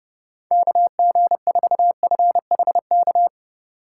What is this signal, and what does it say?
Morse code, heard as a keyed tone, about 39 wpm, 700 Hz. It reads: KG4FHK